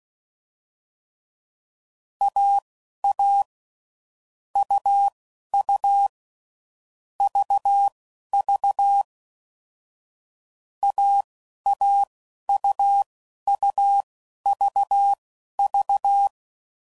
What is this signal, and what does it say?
Morse code, heard as a keyed tone, about 16 wpm, spaced as for 11 wpm, 800 Hz. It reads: AA UU VV AAUUVV
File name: Mixed / Short Runs / AUV AUV4